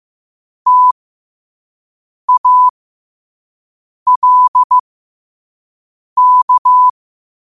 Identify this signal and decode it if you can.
Morse code, heard as a keyed tone, about 15 wpm, spaced as for 5 wpm, 1000 Hz. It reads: TALK
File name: qotdfiles / 5wpm